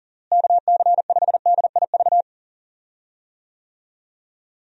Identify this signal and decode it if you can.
Morse code, heard as a keyed tone, about 37 wpm, 700 Hz. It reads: KC5DIV
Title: USA Call Signs - 6 Characters - Repeated 2x - 40wpm